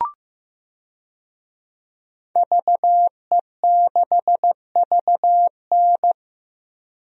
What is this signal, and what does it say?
Morse code, heard as a keyed tone, about 15 wpm, 700 Hz. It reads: VE6VN
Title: DXCC-100 Call Signs - 5 Characters - 15wpm